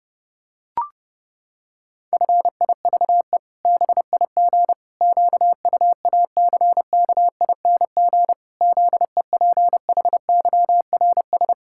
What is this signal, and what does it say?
Morse code, heard as a keyed tone, about 30 wpm, 700 Hz. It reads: FIVE BIG QUACKING ZEPHYRS
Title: Pangram Challenge v2 - 30wpm